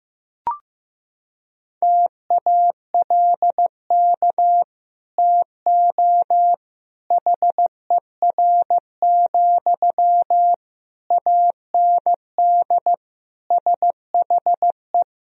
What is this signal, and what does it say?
Morse code, heard as a keyed tone, about 16 wpm, 700 Hz. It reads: TALK TO HER, AND SHE
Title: Sentences from Top 1000 Words - Review Part 3 - 15wpm